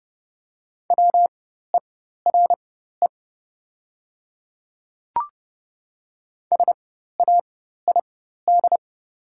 Morse code, heard as a keyed tone, about 30 wpm, spaced as for 14 wpm, 700 Hz. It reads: WERE SAID
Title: 4x Character Spacing - Top 100 Words - 30wpm